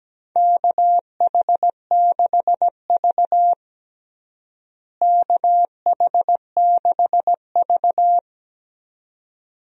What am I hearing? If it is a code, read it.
Morse code, heard as a keyed tone, about 17 wpm, 700 Hz. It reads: KH6V KH6V
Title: DXCC-100 Call Signs - 4 Characters - Repeated 2x - 17wpm